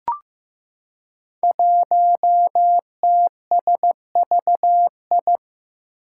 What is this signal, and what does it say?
Morse code, heard as a keyed tone, about 15 wpm, 700 Hz. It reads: 1TSVI
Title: Five Challenging Letter-Number Combinations 15wpm